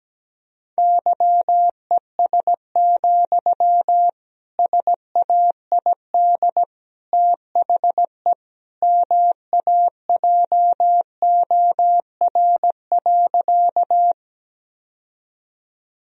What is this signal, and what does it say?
Morse code, heard as a keyed tone, about 17 wpm, 700 Hz. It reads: YES, SAID THE MAJOR.
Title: Sentences from Top 1000 Words - Review Part 4 - 17wpm